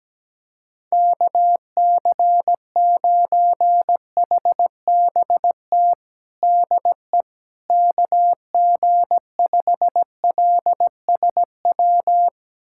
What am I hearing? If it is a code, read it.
Morse code, heard as a keyed tone, about 17 wpm, 700 Hz. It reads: KC9HBT DE KG5LSW
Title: USA Callsign DE Callsign 17wpm